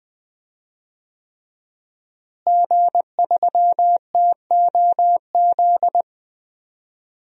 Morse code, heard as a keyed tone, about 20 wpm, 700 Hz. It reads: G3TOZ
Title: DXCC-100 Call Signs - 5 Characters - Repeated 3x - 20wpm